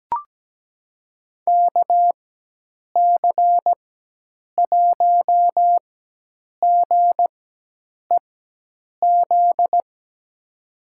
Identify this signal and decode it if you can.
Morse code, heard as a keyed tone, about 17 wpm, spaced as for 8 wpm, 700 Hz. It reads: KC1GEZ
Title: USA Call Signs - 4x Character Spacing - 17wpm